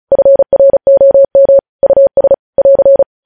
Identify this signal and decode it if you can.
Morse code, heard as a keyed tone, about 36 wpm, 550 Hz. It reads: FROM US <AR>
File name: podcast / News headlines-1x-35x35